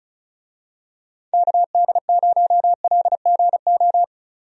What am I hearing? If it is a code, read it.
Morse code, heard as a keyed tone, about 36 wpm, 700 Hz. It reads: KD0LGO
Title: USA Call Signs - 6 Characters - 35wpm